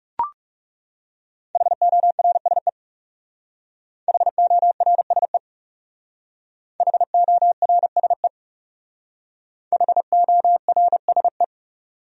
Morse code, heard as a keyed tone, about 37 wpm, 700 Hz. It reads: HORSE HORSE HORSE HORSE
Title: Speed-Racing Top 300 Words 30wpm